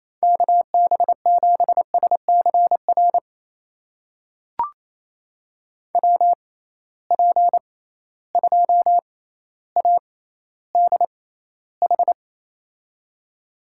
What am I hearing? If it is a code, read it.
Morse code, heard as a keyed tone, about 27 wpm, spaced as for 14 wpm, 700 Hz. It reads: KB7SCR WP2ADH
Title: USA Call Signs - 6 Characters - 6x Character Spacing - 28wpm